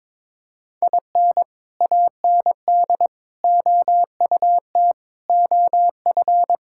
Morse code, heard as a keyed tone, about 22 wpm, 700 Hz. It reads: IN AND OUT OF
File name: practice / 22wpm / Sentences with 4-Word Phrases - 22wpm